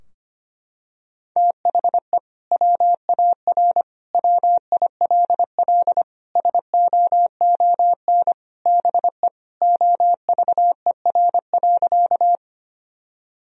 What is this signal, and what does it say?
Morse code, heard as a keyed tone, about 25 wpm, 700 Hz. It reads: THE WAR WILL SOON BE OVER.
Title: Sentences from Top 1000 Words - Review Part 2 - 25wpm